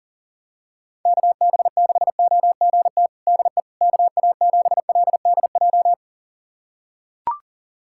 Morse code, heard as a keyed tone, about 39 wpm, 700 Hz. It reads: KB6OGT DE KA7LDJ
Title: USA Callsign DE Callsign - 6 Characters - 40wpm